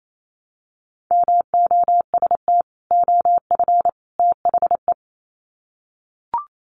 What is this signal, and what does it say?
Morse code, heard as a keyed tone, about 28 wpm, 700 Hz. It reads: MOST OF THE
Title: Sets of 3 Words 28wpm